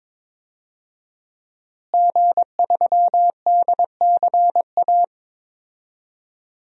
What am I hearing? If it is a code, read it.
Morse code, heard as a keyed tone, about 22 wpm, 700 Hz. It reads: G3DCA
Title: DXCC-100 Call Signs - 5 Characters - Repeated 3x - 22wpm